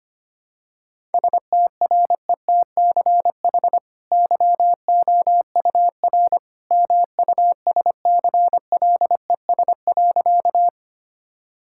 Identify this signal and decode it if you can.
Morse code, heard as a keyed tone, about 25 wpm, 700 Hz. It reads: STRETCH YOUR MUSCLES.
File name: Easy Sentences with Hints - 25wpm